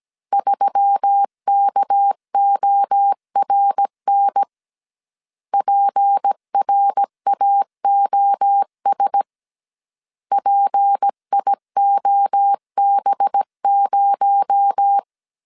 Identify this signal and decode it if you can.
Morse code, heard as a keyed tone, about 17 wpm, 800 Hz. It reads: 3KORN PRAOS PIOB0
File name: CW / DET / 1 6 1500C S A 16wpm 2635